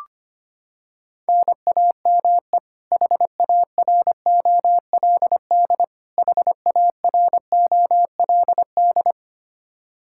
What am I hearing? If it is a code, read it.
Morse code, heard as a keyed tone, about 25 wpm, 700 Hz. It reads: NAME HAROLD HAROLD
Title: K1USN SST QSO Practice 25wpm